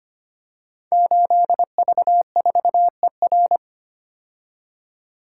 Morse code, heard as a keyed tone, about 25 wpm, 700 Hz. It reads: 8V4ER
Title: Code Groups - 5 Letter:Numbers - 25wpm